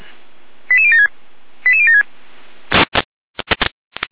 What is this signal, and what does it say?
Five-tone selective call (ZVEI), heard as quick runs of five tones, tones of about 70 ms. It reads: 88076 88076
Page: /455/SDR/ZVEI/89457/2026.02.13/